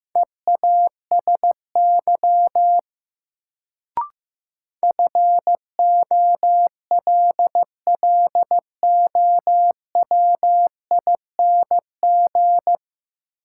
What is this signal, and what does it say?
Morse code, heard as a keyed tone, about 15 wpm, 700 Hz. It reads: EASY FOLLOWING
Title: Sentences from Top 700 Words - New Word Drill - 15wpm